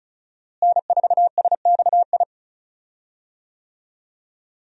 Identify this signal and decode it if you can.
Morse code, heard as a keyed tone, about 36 wpm, 700 Hz. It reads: N4SXI
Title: USA Call Signs - 5 Characters - Repeated 3x - 35wpm